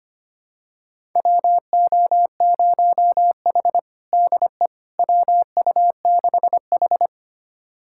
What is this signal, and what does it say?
Morse code, heard as a keyed tone, about 25 wpm, 700 Hz. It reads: WO0H DE WU6H
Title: USA Callsign DE Callsign - 4 Characters - 25wpm